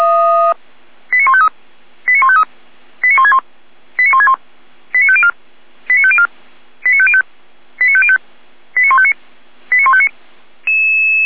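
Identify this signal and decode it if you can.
Five-tone selective call (ZVEI), heard as quick runs of five tones, tones of about 70 ms. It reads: 89153 89153 89161 89161 89504 89504 89505 89505 89159 89159
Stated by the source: ZVEI five-tone call 89170 received by SDR.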